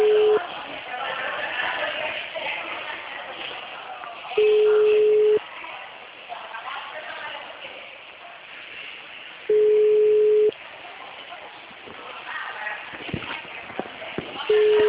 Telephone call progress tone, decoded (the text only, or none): ringback tone